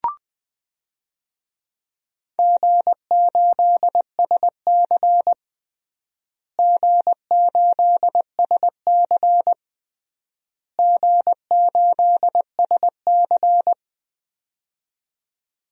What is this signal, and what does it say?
Morse code, heard as a keyed tone, about 20 wpm, 700 Hz. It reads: G8SC G8SC G8SC
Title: DXCC-100 Call Signs - 4 Characters - Repeated 3x - 20wpm